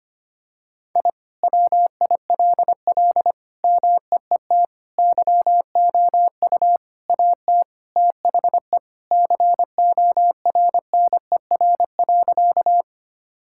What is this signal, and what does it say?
Morse code, heard as a keyed tone, about 25 wpm, 700 Hz. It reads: I WILL MEET YOU AT THE CORNER.
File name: practice / 25wpm / All-Age Appropriate Jokes - 25wpm